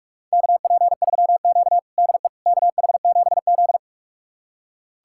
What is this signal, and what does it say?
Morse code, heard as a keyed tone, about 46 wpm, 700 Hz. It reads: KP3Q DE KH7Z
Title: USA Callsign DE Callsign - 4 Characters - 45wpm